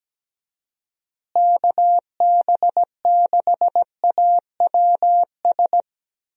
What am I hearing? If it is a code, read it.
Morse code, heard as a keyed tone, about 17 wpm, 700 Hz. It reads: KB6AWS
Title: USA Call Signs - 6 Characters - Repeated 3x - 17wpm